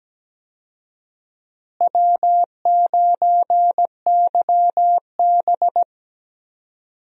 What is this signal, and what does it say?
Morse code, heard as a keyed tone, about 17 wpm, 700 Hz. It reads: W9YB
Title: USA Call Signs - 4 Characters - Repeated 2x - 17wpm